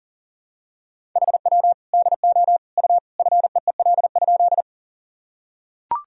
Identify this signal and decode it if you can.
Morse code, heard as a keyed tone, about 40 wpm, 700 Hz. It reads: HW DO U FEEL?
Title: Sentences from Top 400 Words and Abbreviations 40wpm